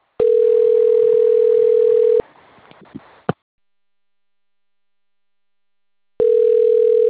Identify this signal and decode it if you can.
Ringback tone (North America), 440 + 480 Hz.